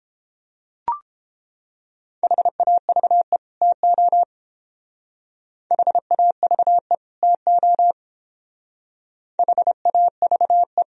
Morse code, heard as a keyed tone, about 28 wpm, 700 Hz. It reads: HAVE TO HAVE TO HAVE